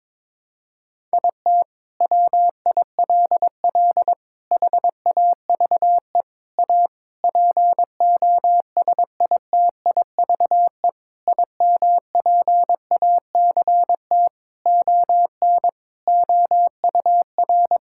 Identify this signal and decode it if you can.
Morse code, heard as a keyed tone, about 22 wpm, 700 Hz. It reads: IT WILL HAVE A POSITIVE IMPACT ON OUR
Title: Sentences from Top 1000 Words - Review Part 6 - 22wpm